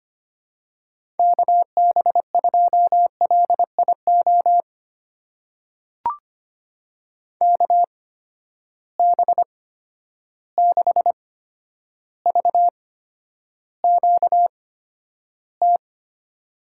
Morse code, heard as a keyed tone, about 25 wpm, spaced as for 11 wpm, 700 Hz. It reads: KB2LIO KB6VQT
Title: USA Call Signs - 6 Characters - 8x Character Spacing - 25wpm